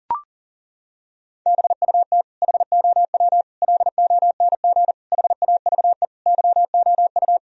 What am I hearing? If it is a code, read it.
Morse code, heard as a keyed tone, about 40 wpm, 700 Hz. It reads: BUT HOW LONG HAVE YOU
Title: Sentences from Top 1000 Words - Review Part 3 - 40wpm